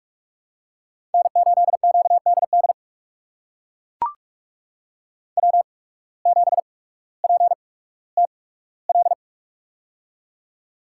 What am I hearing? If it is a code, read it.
Morse code, heard as a keyed tone, about 42 wpm, 690 Hz. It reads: N 8 Q D D W 7 P T L
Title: USA Call Signs - 8x Character Spacing - 45wpm